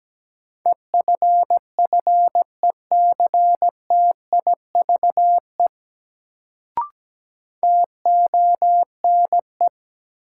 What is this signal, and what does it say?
Morse code, heard as a keyed tone, about 17 wpm, 700 Hz. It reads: EFFECTIVE TONE
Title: Sentences from Top 1600 Words - New Word Drill Alt - 17wpm